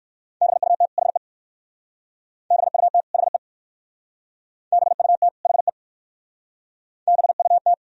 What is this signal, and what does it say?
Morse code, heard as a keyed tone, about 59 wpm, 700 Hz. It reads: BUT HE BUT HE BUT HE BUT